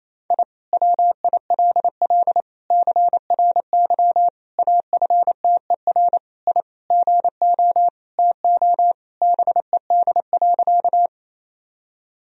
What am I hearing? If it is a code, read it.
Morse code, heard as a keyed tone, about 28 wpm, 700 Hz. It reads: I WILL CRY AFTER I GO TO BED.
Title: Sentences from Top 1000 Words - Review Part 2 - 28wpm